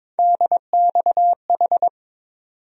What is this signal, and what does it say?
Morse code, heard as a keyed tone, about 22 wpm, 700 Hz. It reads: DXH
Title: USA Call Sign Postfixes - 3 Letters 22wpm